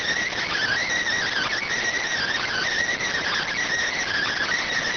Emergency Alert System (SAME) tones, heard as harsh buzzing bursts. It is heard from 0.5 to 4.5 s.